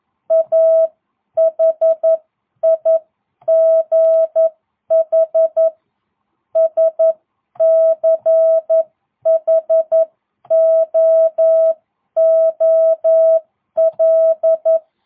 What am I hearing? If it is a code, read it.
Morse code, heard as a keyed tone, about 11 wpm, spaced as for 9 wpm, 650 Hz. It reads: A HIGH SCHOOL